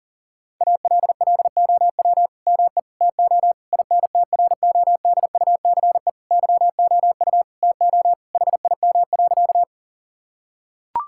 Morse code, heard as a keyed tone, about 40 wpm, 700 Hz. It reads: ALLOW ME TO INTRODUCE YOU TO HIM.
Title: Sentences from Top 1600 Words 40wpm